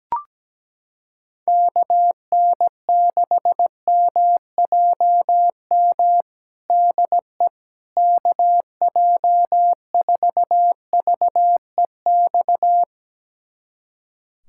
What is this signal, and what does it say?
Morse code, heard as a keyed tone, about 17 wpm, 700 Hz. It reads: KN6MJM DE KJ4VEX
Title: USA Callsign DE Callsign - 6 Characters - 17wpm